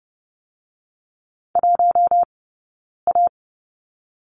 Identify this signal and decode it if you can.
Morse code, heard as a keyed tone, about 30 wpm, 700 Hz. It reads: 1 A
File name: course / 30:10wpm / 086 Lesson 15d - Number 5 - Two Characters 30wpm